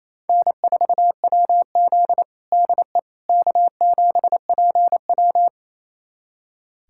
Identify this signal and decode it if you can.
Morse code, heard as a keyed tone, about 27 wpm, 700 Hz. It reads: N4WZ DE K7PW
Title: USA Callsign DE Callsign - 4 Characters - 28wpm